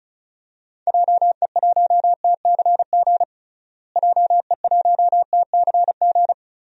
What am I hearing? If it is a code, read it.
Morse code, heard as a keyed tone, about 36 wpm, 700 Hz. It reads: JE1TCG JE1TCG
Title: DXCC-100 Call Signs - 6 Characters - Repeated 3x - 35wpm